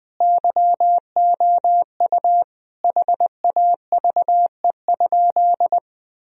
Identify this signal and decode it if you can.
Morse code, heard as a keyed tone, about 21 wpm, 700 Hz. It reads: YOU HAVE?
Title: Sentences from Top 1000 Words - Review Part 3 - 20wpm